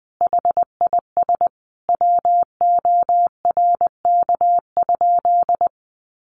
Morse code, heard as a keyed tone, about 20 wpm, 700 Hz. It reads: HIS WORK?
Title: Sentences from Top 1300 Words 20wpm